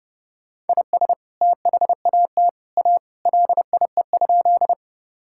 Morse code, heard as a keyed tone, about 30 wpm, 700 Hz. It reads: IS THAT A LIE?
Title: Sentences from Top 1600 Words 30wpm